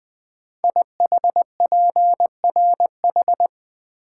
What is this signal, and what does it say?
Morse code, heard as a keyed tone, about 20 wpm, 700 Hz. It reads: IHPRH
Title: Code Groups - 5 Letters - 20wpm